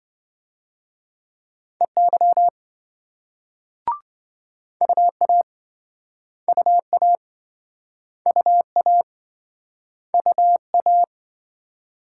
Morse code, heard as a keyed tone, about 28 wpm, 700 Hz. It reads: EY UA UA UA UA